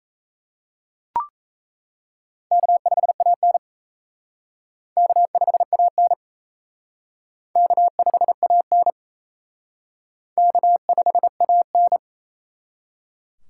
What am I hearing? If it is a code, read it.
Morse code, heard as a keyed tone, about 37 wpm, 700 Hz. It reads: K5AN K5AN K5AN K5AN